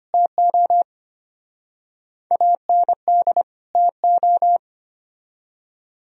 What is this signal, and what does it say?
Morse code, heard as a keyed tone, about 30 wpm, 700 Hz. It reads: TO AND TO
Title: Speed-Racing - Sets of 2 Words 25wpm